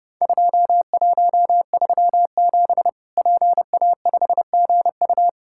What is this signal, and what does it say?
Morse code, heard as a keyed tone, about 30 wpm, 700 Hz. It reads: 2137 PA5GU
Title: Worked All Europe DX Contest - QTC Exchange - 30wpm